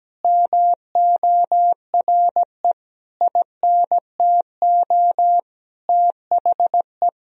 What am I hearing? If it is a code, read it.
Morse code, heard as a keyed tone, about 17 wpm, 700 Hz. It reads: MORE INTO THE
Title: Sentences from Top 1000 Words - Review Part 2 - 17wpm